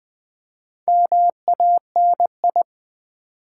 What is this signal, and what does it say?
Morse code, heard as a keyed tone, about 20 wpm, 700 Hz. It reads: MANI
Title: Build A Word - 4 Letters At A Time - 12-Letter Words - 20wpm